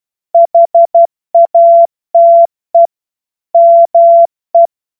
Morse code, heard as a keyed tone, about 12 wpm, 680 Hz. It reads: HATE ME
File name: Morsekurs / Witze